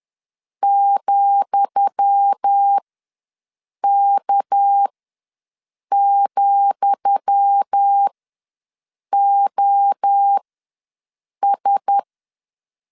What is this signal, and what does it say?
Morse code, heard as a keyed tone, about 10 wpm, spaced as for 6 wpm, 800 Hz. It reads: ,K,OS
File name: LEZIONI / 10 SIGNS / 10D COMA ws 10wpm 4314